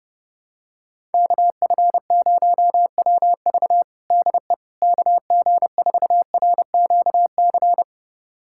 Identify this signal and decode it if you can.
Morse code, heard as a keyed tone, about 30 wpm, 700 Hz. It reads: KF0WV DE KG4RQC
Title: USA Callsign DE Callsign 30wpm